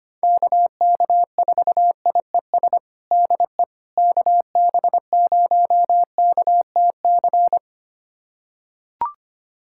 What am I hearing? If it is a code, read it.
Morse code, heard as a keyed tone, about 25 wpm, 700 Hz. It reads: KK4IES DE KB0KTC
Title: USA Callsign DE Callsign 25wpm